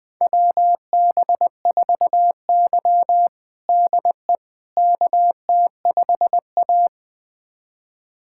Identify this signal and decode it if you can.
Morse code, heard as a keyed tone, about 21 wpm, 700 Hz. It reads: WB4Y DE KT5A
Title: USA Callsign DE Callsign - 4 Characters - 20wpm